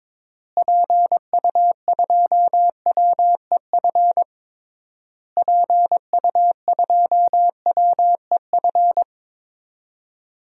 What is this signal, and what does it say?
Morse code, heard as a keyed tone, about 22 wpm, 700 Hz. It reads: PU2WEF PU2WEF
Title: DXCC-100 Call Signs - 6 Characters - Repeated 3x - 22wpm